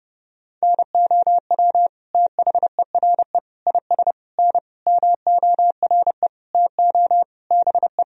Morse code, heard as a keyed tone, about 30 wpm, 700 Hz. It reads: NOW THERE IS N MORE TO BE
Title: Sentences from Top 100 Words and Abbreviations 30wpm